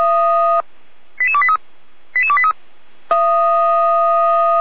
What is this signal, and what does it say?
Five-tone selective call (ZVEI), heard as quick runs of five tones, tones of about 70 ms. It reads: 88283 88283
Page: SDR/ZVEI